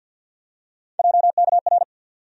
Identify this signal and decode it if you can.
Morse code, heard as a keyed tone, about 53 wpm, 700 Hz. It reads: JKR
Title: USA Call Sign Postfixes - 3 Letters 50wpm